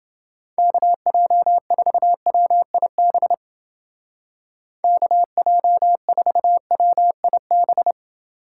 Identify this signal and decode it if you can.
Morse code, heard as a keyed tone, about 29 wpm, 700 Hz. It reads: KJ4WIB KJ4WIB